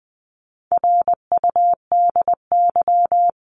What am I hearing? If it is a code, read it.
Morse code, heard as a keyed tone, about 20 wpm, 700 Hz. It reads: RUDY
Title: CWT Exchange 20wpm